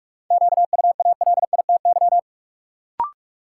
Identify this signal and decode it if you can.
Morse code, heard as a keyed tone, about 43 wpm, 700 Hz. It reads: QUALITY
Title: Sentences from Top 1500 Words - New Word Drill Alt - 45wpm